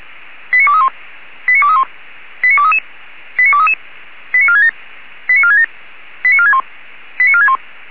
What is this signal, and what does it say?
Five-tone selective call (ZVEI), heard as quick runs of five tones, tones of about 70 ms. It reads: 89231 89231 89233 89233 89467 89467 89461 89461
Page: SDR/ZVEI